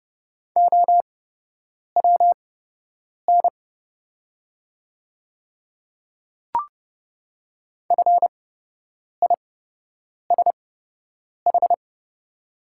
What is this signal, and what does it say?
Morse code, heard as a keyed tone, about 30 wpm, spaced as for 8 wpm, 700 Hz. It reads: OWN FISH
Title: 8x Character Spacing - Top 300 Words - 30wpm